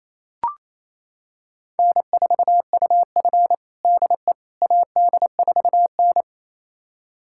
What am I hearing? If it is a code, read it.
Morse code, heard as a keyed tone, about 28 wpm, 700 Hz. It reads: N4UF DE AD4N
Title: USA Callsign DE Callsign - 4 Characters - 28wpm